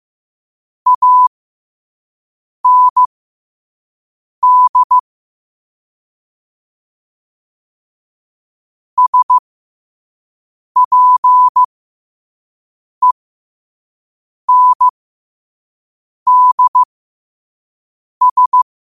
Morse code, heard as a keyed tone, about 15 wpm, spaced as for 5 wpm, 1000 Hz. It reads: AND SPENDS
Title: 5 WPM morse code quotes for Fri, 01 Mar 2024